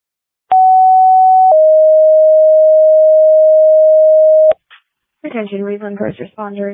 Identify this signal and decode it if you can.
Two-tone page, heard as a one-second tone, then a three-second tone, about 750 then 620 Hz, with speech. It is heard from 0.5 to 4.5 s.